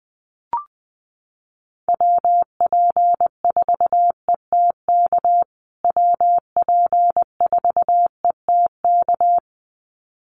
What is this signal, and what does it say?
Morse code, heard as a keyed tone, about 20 wpm, 700 Hz. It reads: WP4ETK WP4ETK
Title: USA Call Signs Repeated 2x 20wpm